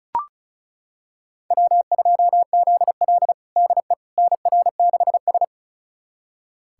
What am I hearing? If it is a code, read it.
Morse code, heard as a keyed tone, about 36 wpm, 700 Hz. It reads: W2ZL DE NR6S
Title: USA Callsign DE Callsign - 4 Characters - 35wpm